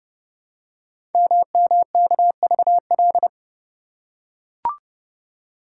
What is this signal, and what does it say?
Morse code, heard as a keyed tone, about 30 wpm, 700 Hz. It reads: MMKVL
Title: Code Groups - 5 Letters - 30wpm